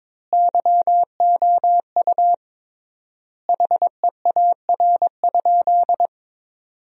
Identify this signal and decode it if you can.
Morse code, heard as a keyed tone, about 22 wpm, 700 Hz. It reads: YOU HEAR?
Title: 3x Word-Spacing - Sentences from Top 300 Words - 22wpm